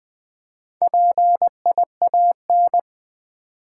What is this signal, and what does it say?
Morse code, heard as a keyed tone, about 20 wpm, 700 Hz. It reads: PIAN
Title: Build A Word - 1 Letter At A Time - 5-Letter Words - 20wpm